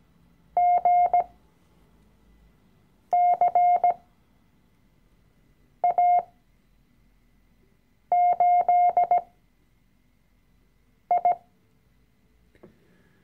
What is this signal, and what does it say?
Morse code, heard as a keyed tone, about 17 wpm, spaced as for 4 wpm, 710 Hz. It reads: GCA8I
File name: CW-Online-Kurs / Anfänger / Lektion 012